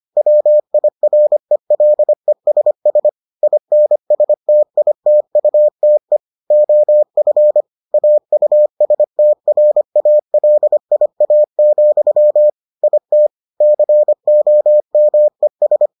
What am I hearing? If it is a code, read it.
Morse code, heard as a keyed tone, about 25 wpm, 600 Hz. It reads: WIRELESS INSTITUTE OF AUSTRALIA, IT COMES